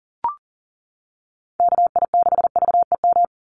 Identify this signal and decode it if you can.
Morse code, heard as a keyed tone, about 39 wpm, 700 Hz. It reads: KI6VEM
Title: USA Call Signs 40wpm